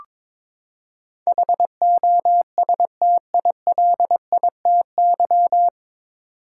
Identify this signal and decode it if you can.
Morse code, heard as a keyed tone, about 22 wpm, 700 Hz. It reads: HOSTILITY
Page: Morse Code Ninja practice 22wpm